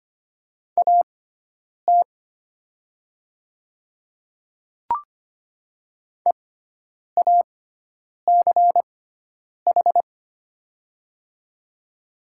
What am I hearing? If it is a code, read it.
Morse code, heard as a keyed tone, about 25 wpm, spaced as for 9 wpm, 700 Hz. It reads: AT EACH
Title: 6x Character Spacing - Top 200 Words - 25wpm